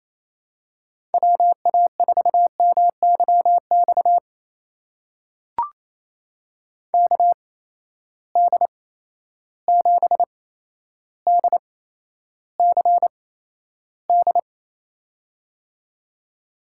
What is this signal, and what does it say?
Morse code, heard as a keyed tone, about 28 wpm, spaced as for 12 wpm, 700 Hz. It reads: WA4MYX KD7DCD
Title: USA Call Signs - 6 Characters - 8x Character Spacing - 28wpm